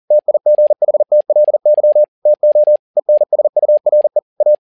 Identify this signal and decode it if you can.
Morse code, heard as a keyed tone, about 40 wpm, 600 Hz. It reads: TIGHTLY TO ENSURE A
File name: learning center / tower trouble / tower touble 40wpm